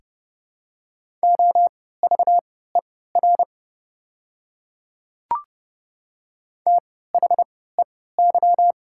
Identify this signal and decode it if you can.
Morse code, heard as a keyed tone, about 30 wpm, spaced as for 17 wpm, 700 Hz. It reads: OVER THEY
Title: 3x Character Spacing - Top 100 Words - 30wpm